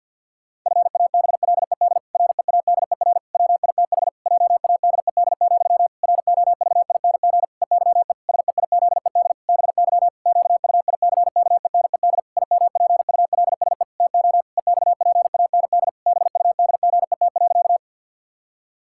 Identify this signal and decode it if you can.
Morse code, heard as a keyed tone, about 51 wpm, 700 Hz. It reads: FABLED READER WITH JADED, ROVING EYE SEIZED BY QUICKENED IMPULSE TO EXPAND BUDGET.
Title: Pangram Challenge v2 - 50wpm